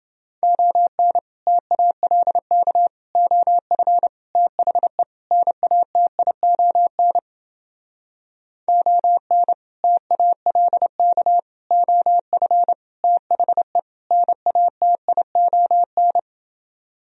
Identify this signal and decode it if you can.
Morse code, heard as a keyed tone, about 29 wpm, 700 Hz. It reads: ON TALK OF THE NATION ON TALK OF THE NATION